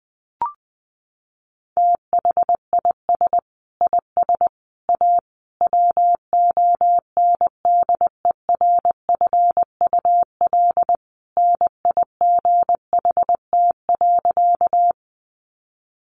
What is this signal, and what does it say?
Morse code, heard as a keyed tone, about 20 wpm, 700 Hz. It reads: THIS IS A WONDERFUL NIGHT.
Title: Sentences from Top 1100 Words 20wpm